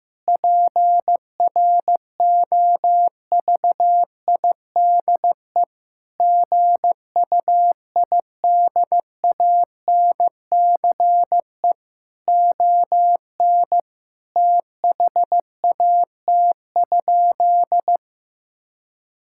Morse code, heard as a keyed tone, about 15 wpm, 700 Hz. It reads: PROVIDE GUIDANCE ON THAT?
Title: Sentences from Top 1600 Words 15wpm